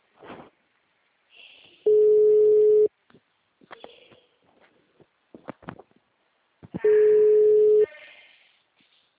Ringback tone (Europe, 425 Hz).